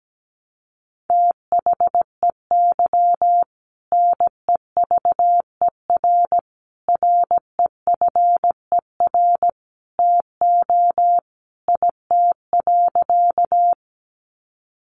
Morse code, heard as a keyed tone, about 17 wpm, 700 Hz. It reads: THEY NEVER REFER TO IT.
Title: Sentences from Top 1000 Words 17wpm